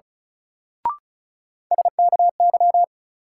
Morse code, heard as a keyed tone, about 34 wpm, 700 Hz. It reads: SKY
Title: Sentences from Top 1200 Words - New Word Drill - 35wpm